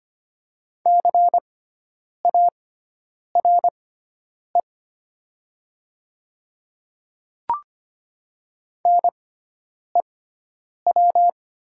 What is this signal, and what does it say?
Morse code, heard as a keyed tone, about 25 wpm, spaced as for 9 wpm, 700 Hz. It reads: CARE NEW
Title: 6x Character Spacing - Top 300 Words - 25wpm